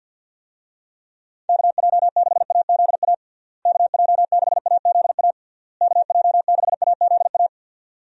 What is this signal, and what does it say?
Morse code, heard as a keyed tone, about 52 wpm, 700 Hz. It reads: KJ6AZA KJ6AZA KJ6AZA
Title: USA Call Signs - 6 Characters - Repeated 3x - 50wpm